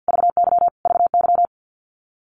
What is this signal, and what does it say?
Morse code, heard as a keyed tone, about 50 wpm, 700 Hz. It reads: VY VY
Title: Abbreviations Repeated 3x - 50wpm